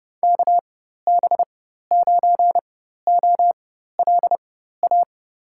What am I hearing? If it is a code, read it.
Morse code, heard as a keyed tone, about 32 wpm, spaced as for 14 wpm, 700 Hz. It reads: KB9OLA
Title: USA Call Signs - 6 Characters - 4x Character Spacing - 30wpm